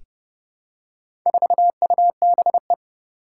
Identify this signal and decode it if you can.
Morse code, heard as a keyed tone, about 30 wpm, 700 Hz. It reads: 4UBE
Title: Four Challenging Letter-Number Combinations 30wpm